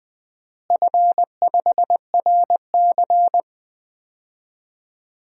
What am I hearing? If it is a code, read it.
Morse code, heard as a keyed tone, about 20 wpm, 700 Hz. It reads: F5RC
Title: DXCC-100 Call Signs - 4 Characters - Repeated 2x - 20wpm